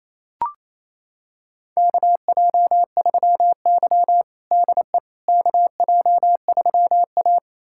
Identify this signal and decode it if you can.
Morse code, heard as a keyed tone, about 28 wpm, 700 Hz. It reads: KJ3Y DE KJ3A
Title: USA Callsign DE Callsign - 4 Characters - 28wpm